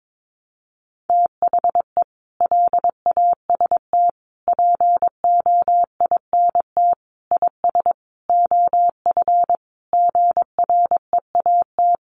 Morse code, heard as a keyed tone, about 22 wpm, 700 Hz. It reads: THE LAST POINT IS OF GREAT
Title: Sentences from Top 1200 Words 22wpm